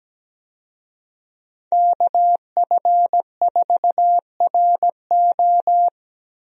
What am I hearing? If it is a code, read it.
Morse code, heard as a keyed tone, about 17 wpm, 700 Hz. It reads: KF4RO
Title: USA Call Signs - 5 Characters - Repeated 2x - 17wpm